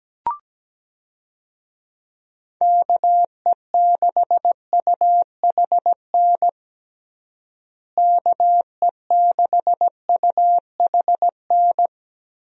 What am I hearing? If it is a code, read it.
Morse code, heard as a keyed tone, about 17 wpm, 700 Hz. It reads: KE6UHN KE6UHN
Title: USA Call Signs - 6 Characters - Repeated 3x - 17wpm